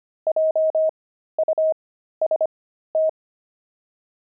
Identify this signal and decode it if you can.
Morse code, heard as a keyed tone, about 25 wpm, spaced as for 13 wpm, 630 Hz. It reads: JUST
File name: pangram-fs-13-cs-25-explorer